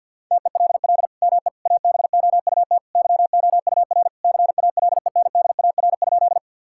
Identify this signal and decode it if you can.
Morse code, heard as a keyed tone, about 50 wpm, 700 Hz. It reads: TELL ME ABOUT YOUR CALENDAR?
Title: Sentences from Top 1600 Words 50wpm